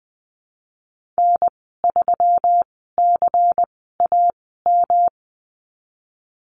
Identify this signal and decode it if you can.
Morse code, heard as a keyed tone, about 20 wpm, spaced as for 14 wpm, 700 Hz. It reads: N3CAM
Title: 231 Lesson 37g - Letter x - Call Signs 20wpm